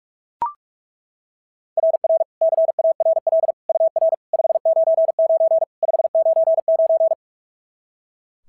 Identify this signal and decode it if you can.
Morse code, heard as a keyed tone, about 39 wpm, 660 Hz. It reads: RR CARL UR 599 599
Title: SOTA QSO Practice 45wpm